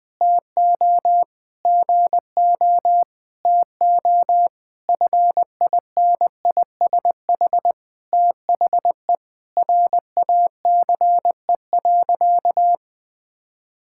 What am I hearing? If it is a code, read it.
Morse code, heard as a keyed tone, about 20 wpm, 700 Hz. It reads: TO GO TO FINISH THE RACE.
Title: Sentences with 8-Word Phrases - 20wpm